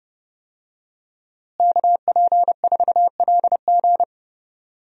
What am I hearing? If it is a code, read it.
Morse code, heard as a keyed tone, about 30 wpm, 700 Hz. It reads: KP4LG
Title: DXCC-100 Call Signs - 5 Characters - Repeated 2x - 30wpm